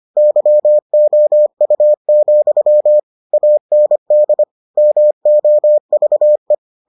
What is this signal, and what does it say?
Morse code, heard as a keyed tone, about 25 wpm, 600 Hz. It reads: YOU, AND MOVE